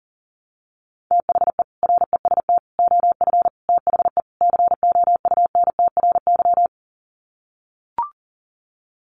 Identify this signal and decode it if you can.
Morse code, heard as a keyed tone, about 37 wpm, 700 Hz. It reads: THE REST OF THE COUNTRY
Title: Sets of 5 Words 40wpm